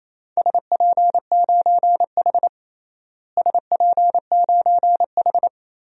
Morse code, heard as a keyed tone, about 28 wpm, 700 Hz. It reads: SP9H SP9H
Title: DXCC-100 Call Signs - 4 Characters - Repeated 3x - 28wpm